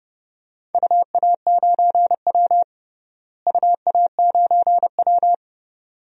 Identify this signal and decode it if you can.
Morse code, heard as a keyed tone, about 30 wpm, 700 Hz. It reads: UA9W UA9W
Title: DXCC-100 Call Signs - 4 Characters - Repeated 2x - 30wpm